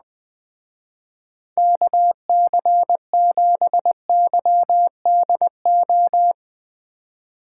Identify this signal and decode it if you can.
Morse code, heard as a keyed tone, about 20 wpm, 700 Hz. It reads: KC7YDO
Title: Speed Racing - USA 2x3 Call Signs 17wpm